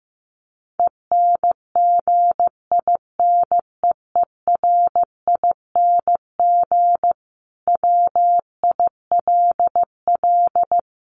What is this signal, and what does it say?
Morse code, heard as a keyed tone, about 15 wpm, 700 Hz. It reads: ENGINEERING WILL
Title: Sentences from Top 1500 Words 15wpm